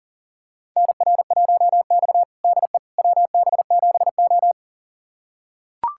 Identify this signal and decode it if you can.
Morse code, heard as a keyed tone, about 37 wpm, 700 Hz. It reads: NR1X DE WB7O
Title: USA Callsign DE Callsign - 4 Characters - 40wpm